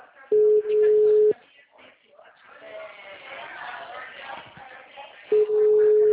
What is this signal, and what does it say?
Ringback tone (Europe, 425 Hz), with speech.